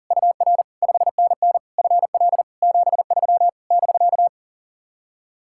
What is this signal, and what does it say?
Morse code, heard as a keyed tone, about 52 wpm, 700 Hz. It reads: UR 5NN FL 73 <BK>